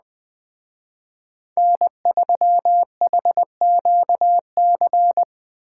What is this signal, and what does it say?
Morse code, heard as a keyed tone, about 20 wpm, 700 Hz. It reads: N3HQC
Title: Speed Racing - USA 1x3 Call Signs 17wpm